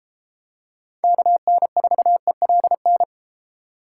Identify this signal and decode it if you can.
Morse code, heard as a keyed tone, about 32 wpm, 700 Hz. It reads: KN4ELN